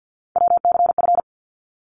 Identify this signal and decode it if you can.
Morse code, heard as a keyed tone, about 50 wpm, 700 Hz. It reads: WCF
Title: ARRL Sections 50wpm